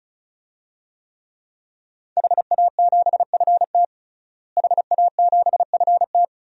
Morse code, heard as a keyed tone, about 36 wpm, 700 Hz. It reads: HA7FT HA7FT
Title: DXCC-100 Call Signs - 5 Characters - Repeated 3x - 35wpm